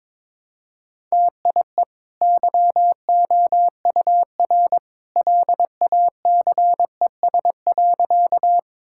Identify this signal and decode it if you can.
Morse code, heard as a keyed tone, about 22 wpm, 690 Hz. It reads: TIE YOUR LACES.
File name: practice / 22wpm / Easy Sentences with Hints - 22wpm